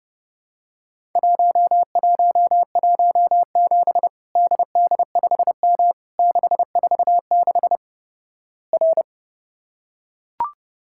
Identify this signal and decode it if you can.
Morse code, heard as a keyed tone, about 29 wpm, 700 Hz. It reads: 1117 DD5M 646 R
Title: Worked All Europe DX Contest - QTC Exchange - 30wpm